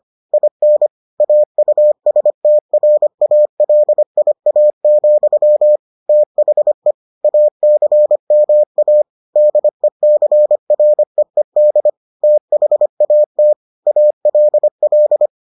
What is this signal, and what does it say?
Morse code, heard as a keyed tone, about 25 wpm, 600 Hz. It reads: IN AUSTRALIA, THE ACMA DECREED THAT ALL